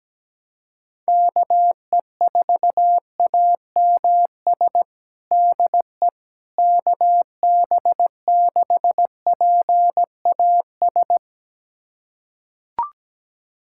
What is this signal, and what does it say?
Morse code, heard as a keyed tone, about 17 wpm, 700 Hz. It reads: KE4AMS DE KB6PAS
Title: USA Callsign DE Callsign 17wpm